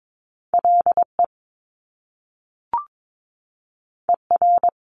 Morse code, heard as a keyed tone, about 22 wpm, 700 Hz. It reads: LE ER
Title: Top Two Letter Combinations in Words 22wpm